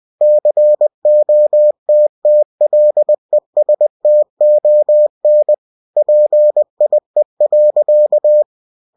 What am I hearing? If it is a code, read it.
Morse code, heard as a keyed tone, about 20 wpm, 600 Hz. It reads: COTTLESTON PIE.
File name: cw-books / winnie-the-pooh / winnie-20wpm-page-0046